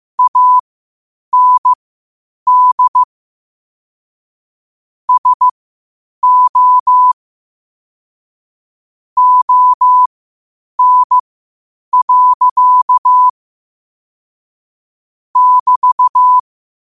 Morse code, heard as a keyed tone, about 15 wpm, spaced as for 8 wpm, 1000 Hz. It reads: AND SO ON. <BT>
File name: qotdfiles / 8wpm